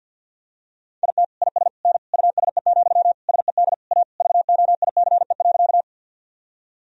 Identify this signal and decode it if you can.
Morse code, heard as a keyed tone, about 50 wpm, 700 Hz. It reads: IT IS N USE, SED A VOICE.
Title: Sentences from Top 500 Words and Abbreviations 50wpm